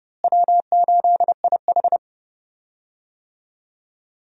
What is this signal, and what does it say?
Morse code, heard as a keyed tone, about 31 wpm, 700 Hz. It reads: W8IH
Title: USA Call Signs - 4 Characters - Repeated 2x - 30wpm